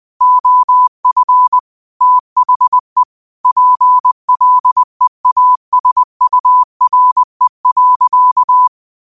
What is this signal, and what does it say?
Morse code, heard as a keyed tone, about 20 wpm, 1000 Hz. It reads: OF THE PLEASURE.
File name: qotdfiles / 20wpm